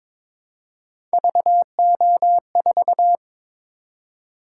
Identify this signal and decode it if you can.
Morse code, heard as a keyed tone, about 22 wpm, 700 Hz. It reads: VO4
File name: practice / 22wpm / Three Letter-Number Combinations - 22wpm